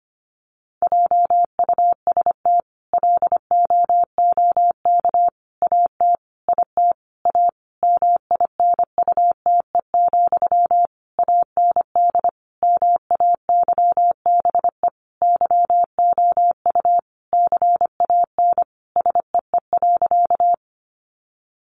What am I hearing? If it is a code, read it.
Morse code, heard as a keyed tone, about 25 wpm, 700 Hz. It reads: JUST LOOK AT IT A MINUTE, AND MAYBE YOU CAN SEE.
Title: Sentences from Top 1000 Words 25wpm